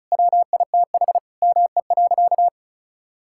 Morse code, heard as a keyed tone, about 34 wpm, 700 Hz. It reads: WITH ME.
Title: Sentences from Top 1000 Words - Review Part 5 - 35wpm